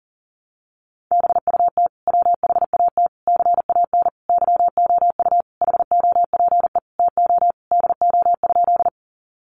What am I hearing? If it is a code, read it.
Morse code, heard as a keyed tone, about 40 wpm, 700 Hz. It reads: BUT WHAT CAN YOU HOPE TO DO?
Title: Sentences from Top 1000 Words 40wpm